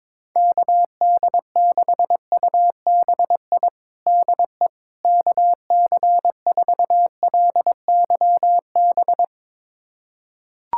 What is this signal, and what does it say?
Morse code, heard as a keyed tone, about 22 wpm, 700 Hz. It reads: KD6UBI DE KC4LYB
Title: USA Callsign DE Callsign - 6 Characters - 22wpm